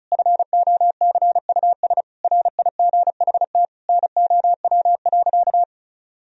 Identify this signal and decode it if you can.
Morse code, heard as a keyed tone, about 36 wpm, 700 Hz. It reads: FOCUS RIGHT NOW.
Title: Sentences from Top 1000 Words - Review Part 4 - 35wpm